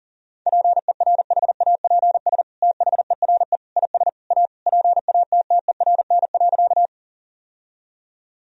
Morse code, heard as a keyed tone, about 40 wpm, 700 Hz. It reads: PERHAPS THERE IS A PATTERN.
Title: Sentences from Top 1000 Words - Review Part 2 - 40wpm